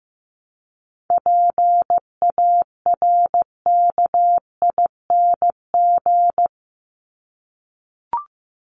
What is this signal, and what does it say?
Morse code, heard as a keyed tone, about 15 wpm, 700 Hz. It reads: PARKING
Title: Top 1501-1750 Words 15wpm